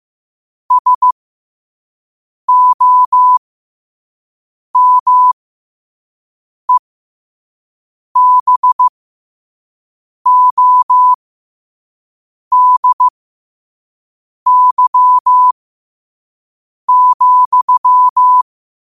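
Morse code, heard as a keyed tone, about 15 wpm, spaced as for 5 wpm, 1000 Hz. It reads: SOMEBODY,
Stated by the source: Morse sent at 5 wpm.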